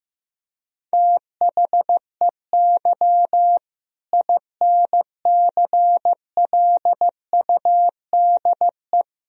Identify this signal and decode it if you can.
Morse code, heard as a keyed tone, about 15 wpm, 700 Hz. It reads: THEY INCLUDE
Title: Sentences from Top 1000 Words - Review Part 3 - 15wpm